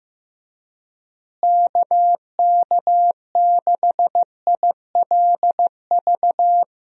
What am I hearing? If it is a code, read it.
Morse code, heard as a keyed tone, about 15 wpm, 700 Hz. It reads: KK6ILV
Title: USA Call Signs - 6 Characters - Repeated 2x - 15wpm